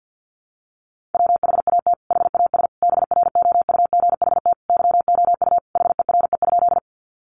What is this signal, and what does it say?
Morse code, heard as a keyed tone, about 50 wpm, 700 Hz. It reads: WHAT HAS BROUGHT YOU HERE?
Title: Sentences from Top 500 Words 50wpm